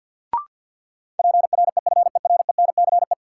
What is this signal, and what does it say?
Morse code, heard as a keyed tone, about 53 wpm, 700 Hz. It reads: PREFERENCE
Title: Sentences from Top 1500 Words - New Word Drill - 50wpm